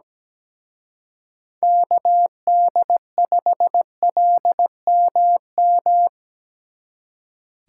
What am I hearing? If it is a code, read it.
Morse code, heard as a keyed tone, about 20 wpm, 700 Hz. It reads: KD5LMM
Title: Speed Racing - USA 2x3 Call Signs 17wpm